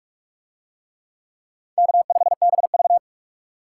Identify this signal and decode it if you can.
Morse code, heard as a keyed tone, about 42 wpm, 700 Hz. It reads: K5BV